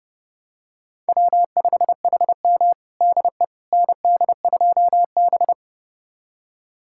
Morse code, heard as a keyed tone, about 29 wpm, 700 Hz. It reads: W5HM DE ND2B
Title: USA Callsign DE Callsign - 4 Characters - 30wpm